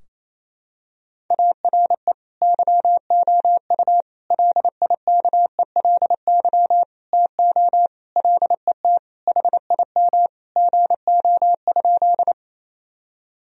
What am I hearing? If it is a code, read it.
Morse code, heard as a keyed tone, about 28 wpm, 700 Hz. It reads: ARE YOU LIKELY TO LET HIM GO?
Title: Sentences from Top 1000 Words - Review Part 4 - 28wpm